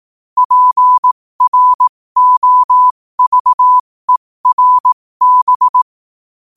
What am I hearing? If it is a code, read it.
Morse code, heard as a keyed tone, about 18 wpm, spaced as for 15 wpm, 1000 Hz. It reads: PROVERB
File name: qotdfiles / 15wpm